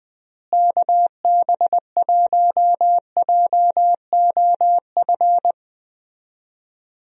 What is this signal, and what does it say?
Morse code, heard as a keyed tone, about 20 wpm, 700 Hz. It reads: KB1JOF
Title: USA Call Signs - 6 Characters - Repeated 2x - 20wpm